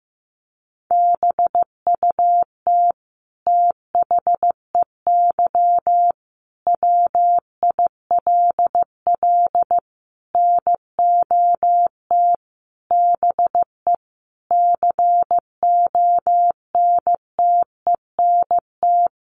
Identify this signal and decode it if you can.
Morse code, heard as a keyed tone, about 15 wpm, 700 Hz. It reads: BUT THEY WILL NOT BE CONTENT
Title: Sentences from Top 900 Words 15wpm